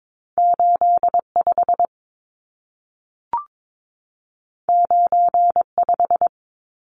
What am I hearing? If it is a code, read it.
Morse code, heard as a keyed tone, about 22 wpm, 700 Hz. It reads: 85 95
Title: Numbers - 2-Digit 22wpm